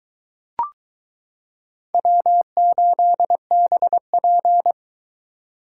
Morse code, heard as a keyed tone, about 22 wpm, 700 Hz. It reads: W8BP